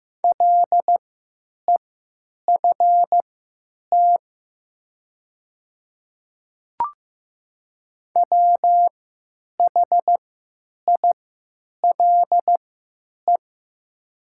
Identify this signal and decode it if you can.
Morse code, heard as a keyed tone, about 15 wpm, spaced as for 9 wpm, 700 Hz. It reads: LEFT WHILE
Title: 3x Character Spacing - Top 300 Words - 15wpm